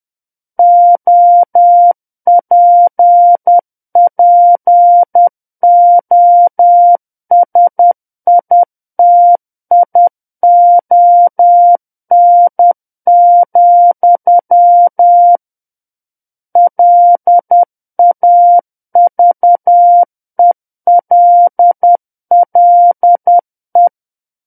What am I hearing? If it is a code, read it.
Morse code, heard as a keyed tone, about 10 wpm, 700 Hz. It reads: OPPOSITION, LAVELLE
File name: Book One / 01-Chapter 1 - 10 WPM